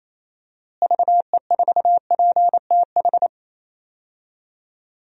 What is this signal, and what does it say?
Morse code, heard as a keyed tone, about 28 wpm, 700 Hz. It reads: VE4PTH
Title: DXCC-100 Call Signs - 6 Characters - Repeated 2x - 28wpm